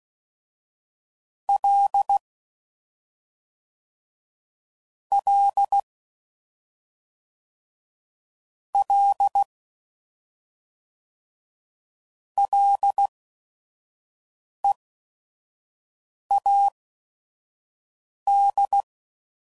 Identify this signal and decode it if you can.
Morse code, heard as a keyed tone, about 16 wpm, spaced as for 5 wpm, 800 Hz. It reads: L L L LEAD
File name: Prefixes / FLs & THs / Single Fs Ls